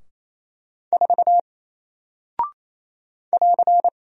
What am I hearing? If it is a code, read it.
Morse code, heard as a keyed tone, about 27 wpm, 700 Hz. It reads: 4 <AR>
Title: Single Character (Letter:Number:Punctuation:Prosign) Alt 28wpm